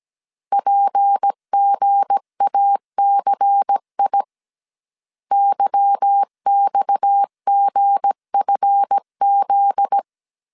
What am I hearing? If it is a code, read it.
Morse code, heard as a keyed tone, about 17 wpm, 800 Hz. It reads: PGACI YXGFZ
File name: CW / DET / 1 9 1500C S A 16wpm 2828